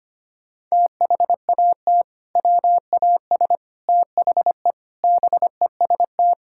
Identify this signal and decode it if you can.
Morse code, heard as a keyed tone, about 25 wpm, 700 Hz. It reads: THAT WAS THE BEST
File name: practice / 25wpm / Sentences with 3-Word Phrases - 25wpm